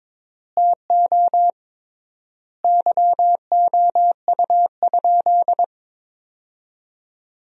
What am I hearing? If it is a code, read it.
Morse code, heard as a keyed tone, about 22 wpm, 700 Hz. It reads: TO YOU?
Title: 3x Word-Spacing - Sentences from Top 500 Words - 22wpm